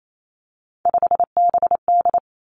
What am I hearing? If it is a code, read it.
Morse code, heard as a keyed tone, about 28 wpm, 700 Hz. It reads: 5BD
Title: Three Challenging Letter-Number Combinations 28wpm